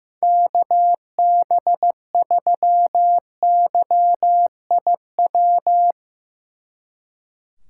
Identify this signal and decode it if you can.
Morse code, heard as a keyed tone, about 16 wpm, 700 Hz. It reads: KB3YIW
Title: Speed Racing - USA 2x3 Call Signs 15wpm